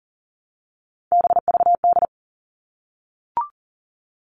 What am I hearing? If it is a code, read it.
Morse code, heard as a keyed tone, about 40 wpm, 700 Hz. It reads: BVD
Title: All Cut Numbers - 3-Digit 40wpm